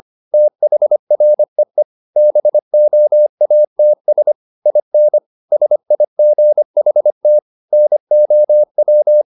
Morse code, heard as a keyed tone, about 25 wpm, 600 Hz. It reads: THREE BOATS IN SIGHT NOW
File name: the old man and the sea part 1 25wpm